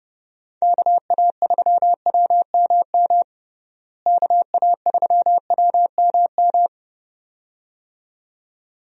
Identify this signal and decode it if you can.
Morse code, heard as a keyed tone, about 30 wpm, 700 Hz. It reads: KA3WMM KA3WMM
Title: USA Call Signs - 6 Characters - Repeated 2x - 30wpm